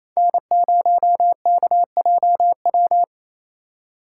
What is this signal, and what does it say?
Morse code, heard as a keyed tone, about 28 wpm, 700 Hz. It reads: N0KJW
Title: USA Call Signs - 5 Characters - 2x Character Spacing - 28wpm